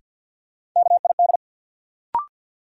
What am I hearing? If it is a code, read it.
Morse code, heard as a keyed tone, about 52 wpm, 700 Hz. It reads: KID
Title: Sets of 5 Words - New Word Drill - 50wpm